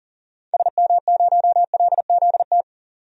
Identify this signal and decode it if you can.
Morse code, heard as a keyed tone, about 39 wpm, 700 Hz. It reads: SM0LZT
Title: DXCC-100 Call Signs - 6 Characters - 40wpm